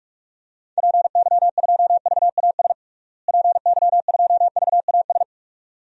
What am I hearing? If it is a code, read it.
Morse code, heard as a keyed tone, about 46 wpm, 700 Hz. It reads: PY2VAS PY2VAS
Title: DXCC-100 Call Signs - 6 Characters - Repeated 2x - 45wpm